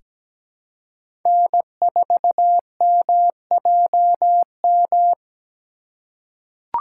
Morse code, heard as a keyed tone, about 17 wpm, 700 Hz. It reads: N4MJM
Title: USA Call Signs - 2x Character Spacing - 17wpm